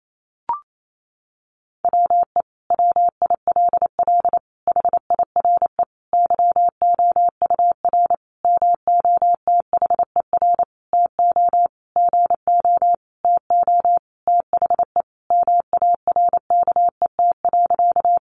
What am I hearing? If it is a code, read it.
Morse code, heard as a keyed tone, about 28 wpm, 700 Hz. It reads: WE WILL HIRE YOUR MOTHER TO GO TO THE MARKET.
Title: Sentences from Top 1400 Words 28wpm